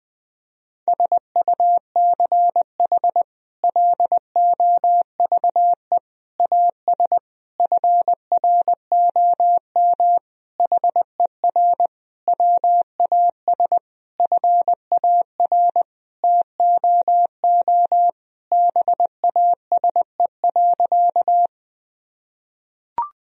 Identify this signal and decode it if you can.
Morse code, heard as a keyed tone, about 21 wpm, 700 Hz. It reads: SUCH LOVE AS FROM HER WAS FAR TOO BASE.
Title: Sentences from Top 1000 Words - Review Part 1 - 20wpm